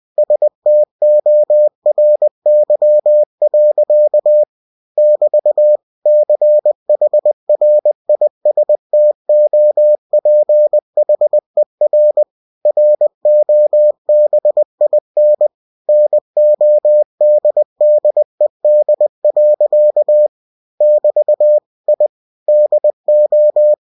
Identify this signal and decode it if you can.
Morse code, heard as a keyed tone, about 20 wpm, 600 Hz. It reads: STORY. <BT> CHRISTOPHER ROBIN NODDED. <BT> I DO
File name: cw-books / winnie-the-pooh / winnie-20wpm-page-0011